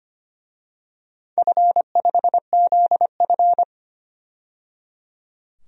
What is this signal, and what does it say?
Morse code, heard as a keyed tone, about 25 wpm, 700 Hz. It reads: F5ZF
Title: DXCC-100 Call Signs - 4 Characters - 25wpm